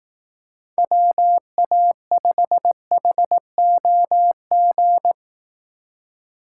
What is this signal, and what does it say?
Morse code, heard as a keyed tone, about 18 wpm, 700 Hz. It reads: WA5HOG
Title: Speed Racing - USA 2x3 Call Signs 15wpm